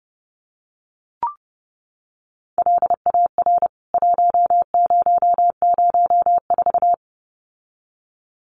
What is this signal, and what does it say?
Morse code, heard as a keyed tone, about 30 wpm, 700 Hz. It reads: LAR 1004
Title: CWT Exchange 30wpm